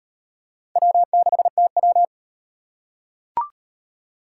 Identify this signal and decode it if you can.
Morse code, heard as a keyed tone, about 38 wpm, 700 Hz. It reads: W6TW